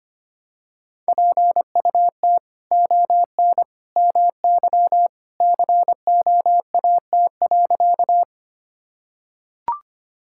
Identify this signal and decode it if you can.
Morse code, heard as a keyed tone, about 25 wpm, 700 Hz. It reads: PUT ON MY COAT.
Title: Sentences from Top 1600 Words 25wpm